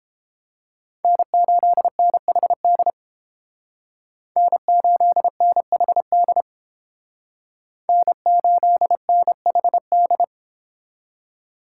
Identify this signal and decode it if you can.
Morse code, heard as a keyed tone, about 30 wpm, 700 Hz. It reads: N8NHD N8NHD N8NHD